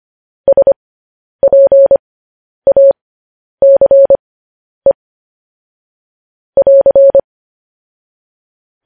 Morse code, heard as a keyed tone, about 25 wpm, spaced as for 10 wpm, 550 Hz. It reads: SPACE <AR>